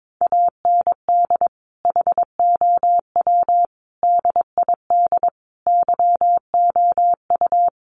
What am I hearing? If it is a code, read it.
Morse code, heard as a keyed tone, about 22 wpm, 700 Hz. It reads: AND HOW DID YOU
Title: Sentences from Top 200 Words 22wpm